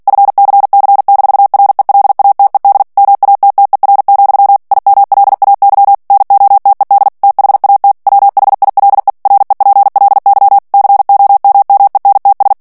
Morse code, heard as a keyed tone, about 48 wpm, 800 Hz. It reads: FCC-RELATED MATTER, IMLAY NOTED THAT WHILE REPLY COMMENTS
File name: ANGLAIS texte clair 3 vitesse 45 mots minute